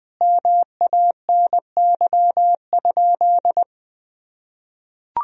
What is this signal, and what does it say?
Morse code, heard as a keyed tone, about 20 wpm, 700 Hz. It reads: MANY?
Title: Sentences from Top 1000 Words - Review Part 1 - 20wpm